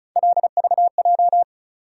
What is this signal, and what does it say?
Morse code, heard as a keyed tone, about 36 wpm, 700 Hz. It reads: LVJ
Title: USA Call Sign Postfixes - 3 Letters 35wpm